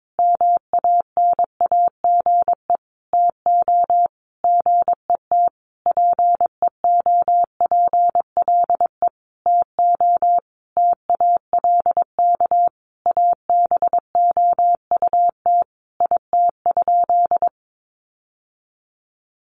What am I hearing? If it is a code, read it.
Morse code, heard as a keyed tone, about 22 wpm, 700 Hz. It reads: MANAGE TO GET PEOPLE TO TALK ABOUT IT?
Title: Sentences from Top 1400 Words 22wpm